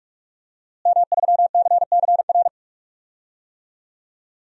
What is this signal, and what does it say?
Morse code, heard as a keyed tone, about 42 wpm, 700 Hz. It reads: M3CCR
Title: DXCC-100 Call Signs - 5 Characters - Repeated 2x - 45wpm